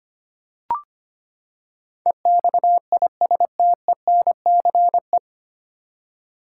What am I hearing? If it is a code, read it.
Morse code, heard as a keyed tone, about 25 wpm, 700 Hz. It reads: EXISTENCE
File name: Nine Letter Words - 25wpm